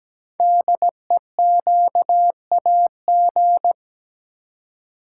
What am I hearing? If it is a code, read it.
Morse code, heard as a keyed tone, about 17 wpm, 700 Hz. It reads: DEQAG
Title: Code Groups - 5 Letters - 17wpm